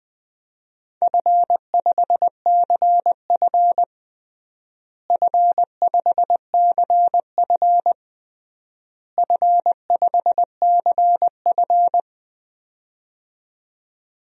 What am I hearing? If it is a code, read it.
Morse code, heard as a keyed tone, about 21 wpm, 700 Hz. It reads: F5CF F5CF F5CF
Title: DXCC-100 Call Signs - 4 Characters - Repeated 3x - 20wpm